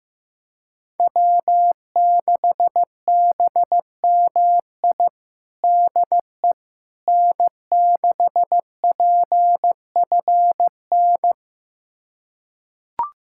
Morse code, heard as a keyed tone, about 15 wpm, 700 Hz. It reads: W6BMI DE N6PFN
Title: USA Callsign DE Callsign - 5 Characters - 15wpm